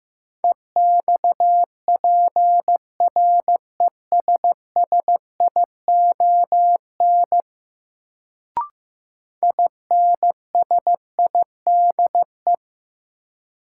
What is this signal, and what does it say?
Morse code, heard as a keyed tone, about 15 wpm, 700 Hz. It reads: EXPRESSION INSIDE
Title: Sentences from Top 1200 Words - New Word Drill Alt - 15wpm